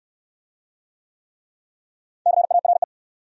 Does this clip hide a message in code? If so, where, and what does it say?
Yes: Morse code from 2.3 to 2.8 s, heard as a keyed tone, about 68 wpm, 690 Hz. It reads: CARE